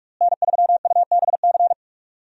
Morse code, heard as a keyed tone, about 46 wpm, 700 Hz. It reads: N3UBC
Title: USA Call Signs - 5 Characters - Repeated 3x - 45wpm